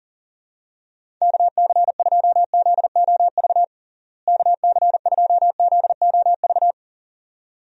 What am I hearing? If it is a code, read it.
Morse code, heard as a keyed tone, about 40 wpm, 700 Hz. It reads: KC2ZOV KC2ZOV
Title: USA Call Signs - 6 Characters - Repeated 2x - 40wpm